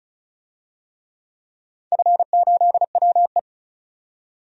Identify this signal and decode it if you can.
Morse code, heard as a keyed tone, about 36 wpm, 700 Hz. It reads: F8WE
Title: DXCC-100 Call Signs - 4 Characters - 35wpm